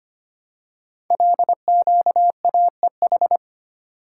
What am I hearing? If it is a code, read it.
Morse code, heard as a keyed tone, about 25 wpm, 700 Hz. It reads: LQAEH
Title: Code Groups - 5 Letters - 25wpm